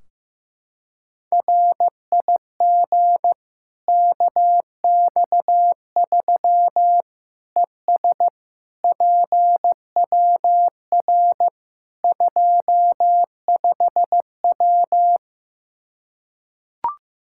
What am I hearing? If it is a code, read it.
Morse code, heard as a keyed tone, about 15 wpm, 700 Hz. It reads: RIG KX3 ES PWR 25W
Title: Practice QSO - Exchange 2 - Info - 15wpm